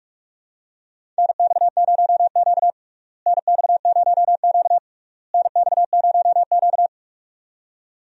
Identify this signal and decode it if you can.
Morse code, heard as a keyed tone, about 42 wpm, 700 Hz. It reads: NX0Q NX0Q NX0Q
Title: USA Call Signs - 4 Characters - Repeated 3x - 45wpm